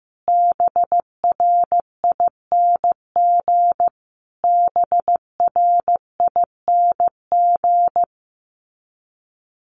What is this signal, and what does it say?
Morse code, heard as a keyed tone, about 15 wpm, 700 Hz. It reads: BRING BRING
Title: Top 500 Words Repeated 15wpm